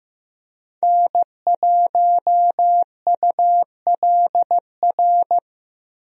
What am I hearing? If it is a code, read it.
Morse code, heard as a keyed tone, about 15 wpm, 700 Hz. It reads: N1ULR
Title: USA Call Signs - 5 Characters - Repeated 2x - 15wpm